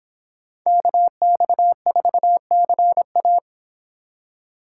Morse code, heard as a keyed tone, about 28 wpm, 700 Hz. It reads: KX4CA